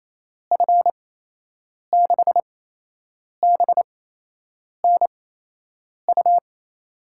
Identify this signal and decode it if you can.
Morse code, heard as a keyed tone, about 28 wpm, spaced as for 8 wpm, 700 Hz. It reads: F6BNU
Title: USA Call Signs - 6 Characters - 8x Character Spacing - 28wpm